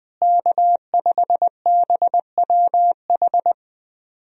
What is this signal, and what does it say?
Morse code, heard as a keyed tone, about 21 wpm, 700 Hz. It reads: K5BWH
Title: USA Call Signs - 5 Characters - Repeated 2x - 20wpm